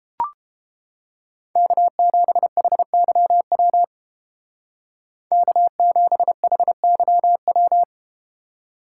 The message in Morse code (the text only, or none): K7HYW K7HYW